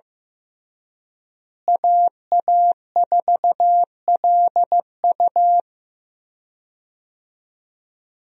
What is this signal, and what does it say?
Morse code, heard as a keyed tone, about 15 wpm, 700 Hz. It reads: AA4LU
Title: USA Call Signs - 5 Characters - Repeated 3x - 15wpm